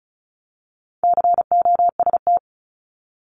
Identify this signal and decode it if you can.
Morse code, heard as a keyed tone, about 36 wpm, 700 Hz. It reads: COST
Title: Multi-letter Phonemes Repeated 3x 35wpm